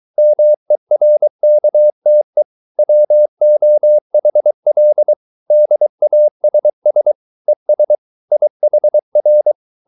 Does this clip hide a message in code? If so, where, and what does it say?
Yes: Morse code from 0.2 to 9.5 s, heard as a keyed tone, about 23 wpm, 600 Hz. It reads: MERKTE WOHL DASS ES IHR